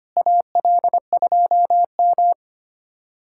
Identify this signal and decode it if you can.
Morse code, heard as a keyed tone, about 25 wpm, 700 Hz. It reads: AL2M
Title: USA Callsign DE Callsign 25wpm